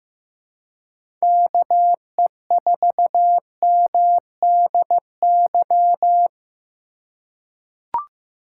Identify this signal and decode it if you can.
Morse code, heard as a keyed tone, about 15 wpm, 700 Hz. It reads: KE4MDY
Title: USA Call Signs - 4x Character Spacing - 15wpm